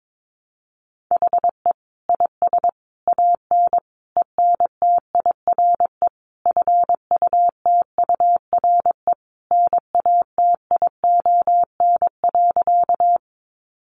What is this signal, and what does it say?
Morse code, heard as a keyed tone, about 22 wpm, 700 Hz. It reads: HE IS AN ENTIRE FUTURE NATION.
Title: A Sound Of Thunder 22wpm